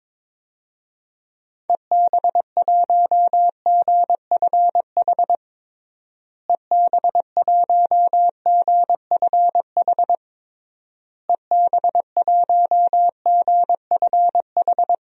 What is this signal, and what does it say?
Morse code, heard as a keyed tone, about 21 wpm, 700 Hz. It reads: EB1GFH EB1GFH EB1GFH
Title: DXCC-100 Call Signs - 6 Characters - Repeated 3x - 22wpm